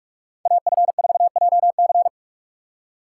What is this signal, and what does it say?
Morse code, heard as a keyed tone, about 46 wpm, 700 Hz. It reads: AF4JC